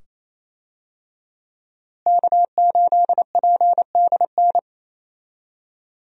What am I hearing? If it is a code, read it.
Morse code, heard as a keyed tone, about 28 wpm, 700 Hz. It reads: K8PDN
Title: USA Call Signs - 5 Characters - Repeated 2x - 28wpm